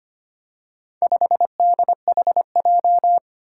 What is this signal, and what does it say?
Morse code, heard as a keyed tone, about 25 wpm, 700 Hz. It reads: 5DHJ
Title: Four Challenging Letter-Number Combinations 25wpm